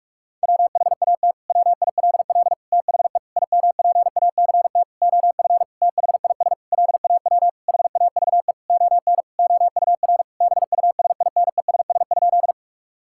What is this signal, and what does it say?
Morse code, heard as a keyed tone, about 41 wpm, 700 Hz. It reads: WHAT WILL THE IMPACT OF THIS LAW HAVE ON OUR BUSINESS?
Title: Sentences from Top 1000 Words - Review Part 6 - 45wpm